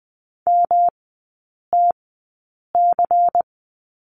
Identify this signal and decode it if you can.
Morse code, heard as a keyed tone, about 20 wpm, spaced as for 8 wpm, 690 Hz. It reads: MTC
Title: 145 Lesson 24e - Letter y - Three Characters 20wpm